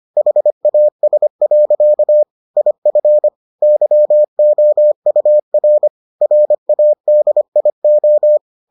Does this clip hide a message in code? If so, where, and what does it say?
Yes: Morse code from 0.2 to 8.4 s, heard as a keyed tone, about 25 wpm, 600 Hz. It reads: HAS. IF YOUR RADIO